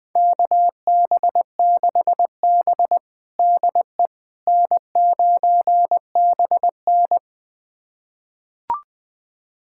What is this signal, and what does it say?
Morse code, heard as a keyed tone, about 20 wpm, 700 Hz. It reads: KB6B DE N9BN
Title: USA Callsign DE Callsign - 4 Characters - 20wpm